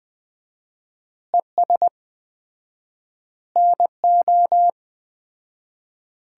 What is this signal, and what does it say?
Morse code, heard as a keyed tone, about 21 wpm, 700 Hz. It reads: ES NO
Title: Common QSO Phrases - 4x Word-Spacing - 20wpm